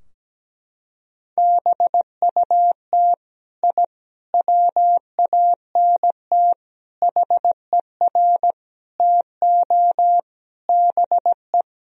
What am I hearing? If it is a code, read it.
Morse code, heard as a keyed tone, about 17 wpm, 700 Hz. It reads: BUT I WANT HER TO BE
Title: Sentences from Top 1000 Words - Review Part 4 - 17wpm